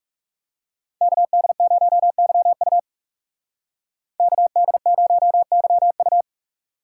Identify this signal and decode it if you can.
Morse code, heard as a keyed tone, about 36 wpm, 700 Hz. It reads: KD0YU KD0YU
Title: Speed Racing - USA 2x2 Call Signs 30wpm